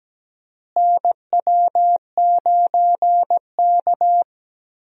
Morse code, heard as a keyed tone, about 17 wpm, 700 Hz. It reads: NW9K
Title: USA Call Signs - 4 Characters - Repeated 2x - 17wpm